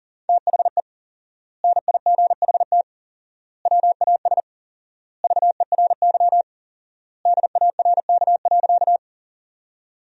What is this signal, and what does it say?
Morse code, heard as a keyed tone, about 40 wpm, 700 Hz. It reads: THE NIGHT WAS VERY DARK.
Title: 4x Word-Spacing - Sentences from Top 500 Words - 40wpm